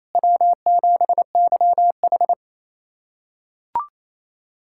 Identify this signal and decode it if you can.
Morse code, heard as a keyed tone, about 28 wpm, 700 Hz. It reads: W7YH
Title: USA Callsign DE Callsign - 4 Characters - 28wpm